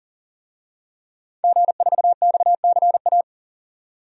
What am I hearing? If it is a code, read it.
Morse code, heard as a keyed tone, about 40 wpm, 700 Hz. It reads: G4XCA
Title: DXCC-100 Call Signs - 5 Characters - 40wpm